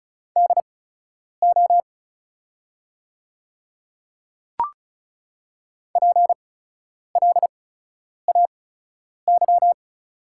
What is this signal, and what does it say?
Morse code, heard as a keyed tone, about 35 wpm, spaced as for 10 wpm, 690 Hz. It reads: DO PLAY
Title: 8x Character Spacing - Top 200 Words - 35wpm